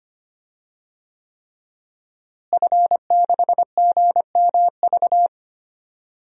Morse code, heard as a keyed tone, about 25 wpm, 700 Hz. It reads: F6GMV
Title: DXCC-100 Call Signs - 5 Characters - Repeated 3x - 25wpm